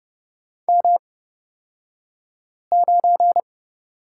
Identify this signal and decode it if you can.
Morse code, heard as a keyed tone, about 30 wpm, spaced as for 5 wpm, 700 Hz. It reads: M9
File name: Lesson 37d - Letter x - Two Characters 30wpm